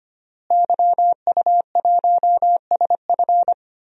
Morse code, heard as a keyed tone, about 25 wpm, 700 Hz. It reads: YU1SF
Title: DXCC-100 Call Signs - 5 Characters - Repeated 2x - 25wpm